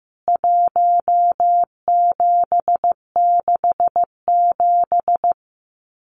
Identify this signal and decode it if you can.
Morse code, heard as a keyed tone, about 15 wpm, 700 Hz. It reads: 1767
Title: CWT Exchange 15wpm